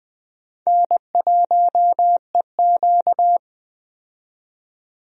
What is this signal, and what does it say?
Morse code, heard as a keyed tone, about 21 wpm, 700 Hz. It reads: N1EQ
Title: USA Call Signs - 4 Characters - 20wpm